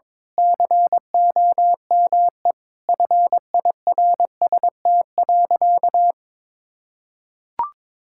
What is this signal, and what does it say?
Morse code, heard as a keyed tone, about 22 wpm, 700 Hz. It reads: COME FIRST.
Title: Sentences from Top 1000 Words - Review Part 6 - 22wpm